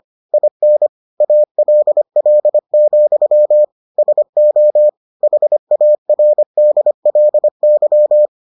Morse code, heard as a keyed tone, about 25 wpm, 600 Hz. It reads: IN ALL, SO HARDLY